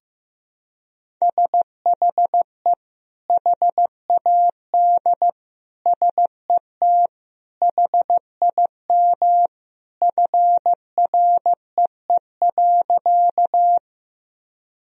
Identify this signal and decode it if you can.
Morse code, heard as a keyed tone, about 15 wpm, 700 Hz. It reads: SHE HAD SET HIM FREE.
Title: Sentences from Top 1000 Words - Review Part 1 - 15wpm